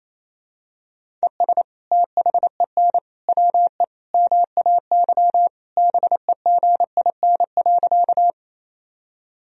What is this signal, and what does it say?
Morse code, heard as a keyed tone, about 27 wpm, 700 Hz. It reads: ES THEN WE MAY BEGIN.
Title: Sentences from Top 300 Words and Abbreviations 28wpm